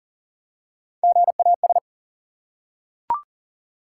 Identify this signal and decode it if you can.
Morse code, heard as a keyed tone, about 40 wpm, 700 Hz. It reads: GAS
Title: Three Letter Words - 40wpm